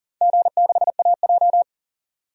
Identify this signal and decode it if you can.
Morse code, heard as a keyed tone, about 40 wpm, 700 Hz. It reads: G6AJ
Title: DXCC-100 Call Signs - 4 Characters - 40wpm